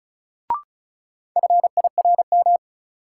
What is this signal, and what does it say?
Morse code, heard as a keyed tone, about 36 wpm, 700 Hz. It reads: FIRM
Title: Sentences from Top 1400 Words - New Word Drill - 35wpm